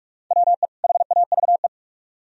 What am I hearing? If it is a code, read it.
Morse code, heard as a keyed tone, about 43 wpm, 700 Hz. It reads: WE HAVE